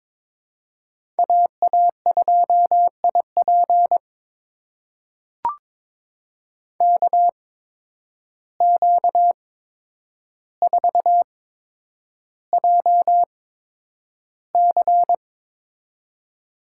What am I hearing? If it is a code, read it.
Morse code, heard as a keyed tone, about 22 wpm, spaced as for 9 wpm, 700 Hz. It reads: AA2IP KQ4JC
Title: USA Call Signs - 5 Characters - 8x Character Spacing - 22wpm